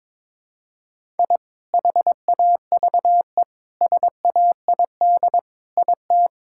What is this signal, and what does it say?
Morse code, heard as a keyed tone, about 21 wpm, 700 Hz. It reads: I HAVE SAID IT
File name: practice / 22wpm / Sentences with 8-Word Phrases - 22wpm